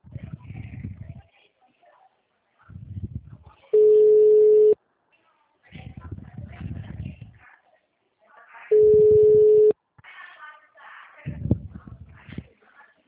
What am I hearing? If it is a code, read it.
Ringback tone (Europe, 425 Hz).